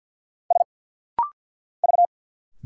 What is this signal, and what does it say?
Morse code, heard as a keyed tone, about 50 wpm, 700 Hz. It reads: S V
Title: Single Letters - Rapid-Fire 50wpm